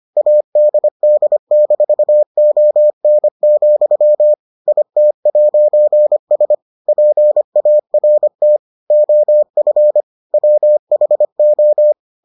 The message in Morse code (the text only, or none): ADD-ON, IT'S PART OF WHO